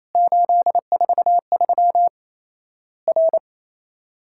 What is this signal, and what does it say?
Morse code, heard as a keyed tone, about 28 wpm, 700 Hz. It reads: 843 R
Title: Worked All Europe DX Contest - QTC Exchange - 28wpm